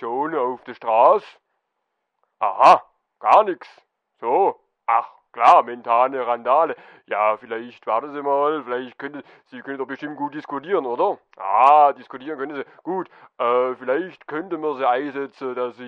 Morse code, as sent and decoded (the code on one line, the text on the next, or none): none